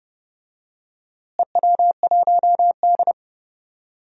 Morse code, heard as a keyed tone, about 29 wpm, 700 Hz. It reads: EW1D
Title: DXCC-100 Call Signs - 4 Characters - Repeated 3x - 30wpm